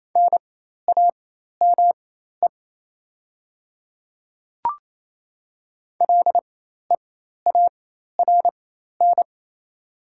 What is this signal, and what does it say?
Morse code, heard as a keyed tone, about 28 wpm, spaced as for 13 wpm, 700 Hz. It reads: NAME LEARN
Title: 4x Character Spacing - Top 500 Words - 28wpm